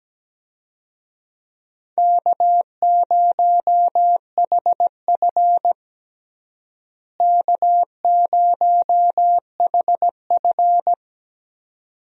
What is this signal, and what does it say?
Morse code, heard as a keyed tone, about 17 wpm, 700 Hz. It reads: K0HF K0HF
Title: USA Call Signs - 4 Characters - Repeated 2x - 17wpm